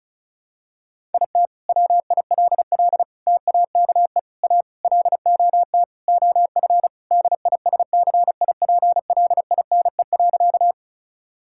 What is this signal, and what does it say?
Morse code, heard as a keyed tone, about 36 wpm, 700 Hz. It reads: IT WILL TAKE A LOT OF DISCIPLINE.
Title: Sentences from Top 1000 Words - Review Part 6 - 35wpm